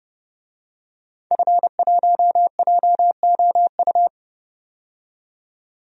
Morse code, heard as a keyed tone, about 32 wpm, 700 Hz. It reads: F1JOU
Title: DXCC-100 Call Signs - 5 Characters - 30wpm